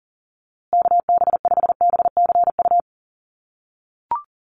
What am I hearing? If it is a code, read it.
Morse code, heard as a keyed tone, about 40 wpm, 700 Hz. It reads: KB5BCU
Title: USA Call Signs Repeated 2x 40wpm